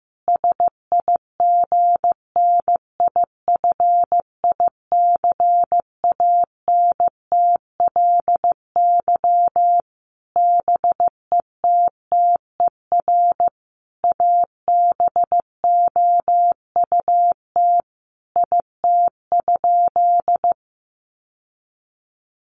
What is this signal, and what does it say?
Morse code, heard as a keyed tone, about 15 wpm, 700 Hz. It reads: SIGNIFICANTLY BETTER ABOUT IT?
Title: Sentences from Top 1500 Words 15wpm